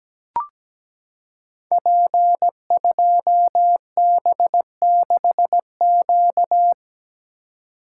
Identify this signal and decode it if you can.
Morse code, heard as a keyed tone, about 17 wpm, 700 Hz. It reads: P2B6Q
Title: Code Groups - 5 Letter:Numbers - 17wpm